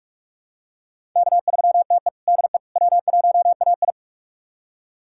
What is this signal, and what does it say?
Morse code, heard as a keyed tone, about 45 wpm, 700 Hz. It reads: K3TE DE W1AI
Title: USA Callsign DE Callsign - 4 Characters - 45wpm